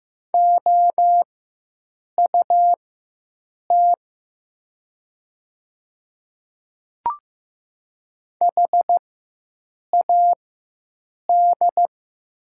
Morse code, heard as a keyed tone, about 15 wpm, spaced as for 7 wpm, 700 Hz. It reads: OUT HAD
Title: 4x Character Spacing - Top 200 Words - 15wpm